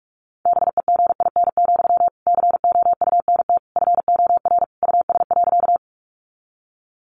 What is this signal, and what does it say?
Morse code, heard as a keyed tone, about 46 wpm, 700 Hz. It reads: BEGIN, COUNT FOR US.
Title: Sentences from Top 1500 Words 45wpm